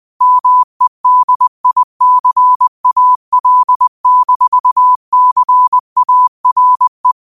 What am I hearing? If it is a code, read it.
Morse code, heard as a keyed tone, about 20 wpm, 1000 Hz. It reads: MEDICAL-CARE
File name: qotdfiles / 20wpm